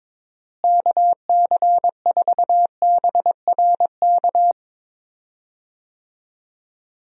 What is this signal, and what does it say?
Morse code, heard as a keyed tone, about 22 wpm, 700 Hz. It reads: KC4BRK
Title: USA Call Signs - 6 Characters - Repeated 2x - 22wpm